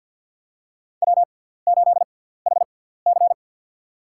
Morse code, heard as a keyed tone, about 52 wpm, spaced as for 17 wpm, 700 Hz. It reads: W8HC
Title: USA Call Signs - 4 Characters - 6x Character Spacing - 50wpm